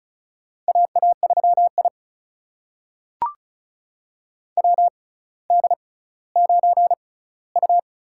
Morse code, heard as a keyed tone, about 36 wpm, spaced as for 18 wpm, 700 Hz. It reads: AA3I WD9U
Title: USA Call Signs - 4 Characters - 6x Character Spacing - 35wpm